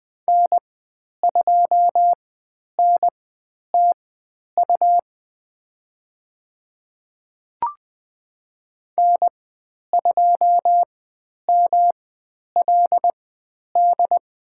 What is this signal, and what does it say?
Morse code, heard as a keyed tone, about 20 wpm, spaced as for 10 wpm, 700 Hz. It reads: N2NTU N2MLD
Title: 109 Lesson 18g - Letter m - Call Signs 20wpm